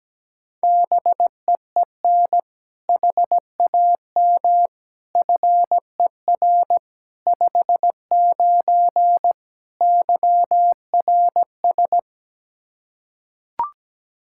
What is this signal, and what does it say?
Morse code, heard as a keyed tone, about 17 wpm, 700 Hz. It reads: BEEN HAM FER 59 YRS
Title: Practice QSO - Exchange 3 - More Info - 17wpm